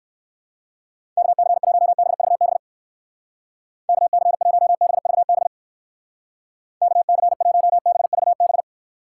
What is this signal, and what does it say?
Morse code, heard as a keyed tone, about 68 wpm, 700 Hz. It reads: KC1BVB KC1BVB KC1BVB